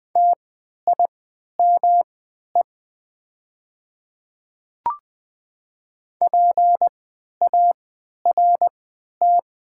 Morse code, heard as a keyed tone, about 20 wpm, spaced as for 11 wpm, 700 Hz. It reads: TIME PART
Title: 3x Character Spacing - Top 500 Words - 20wpm